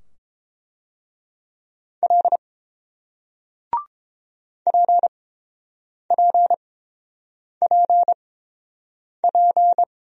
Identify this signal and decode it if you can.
Morse code, heard as a keyed tone, about 30 wpm, spaced as for 7 wpm, 700 Hz. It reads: L PPPP